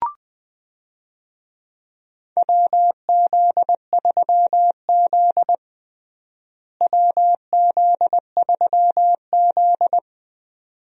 Morse code, heard as a keyed tone, about 20 wpm, 700 Hz. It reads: WZ3Z WZ3Z
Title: USA Call Signs - 4 Characters - Repeated 2x - 20wpm